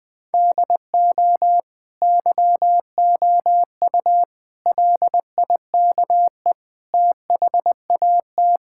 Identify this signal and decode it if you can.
Morse code, heard as a keyed tone, about 20 wpm, 700 Hz. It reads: DO YOU LIKE THAT
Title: Sentences from Top 1000 Words - Review Part 5 - 20wpm